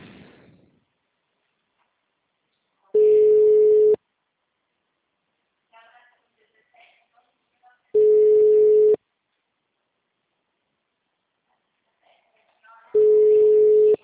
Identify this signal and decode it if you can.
Ringback tone (Europe, 425 Hz), with speech.